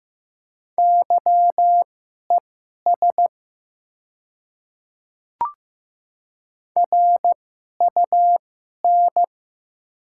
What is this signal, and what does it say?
Morse code, heard as a keyed tone, about 15 wpm, spaced as for 11 wpm, 700 Hz. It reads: YES RUN
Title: 2x Character Spacing - Top 500 Words - 15wpm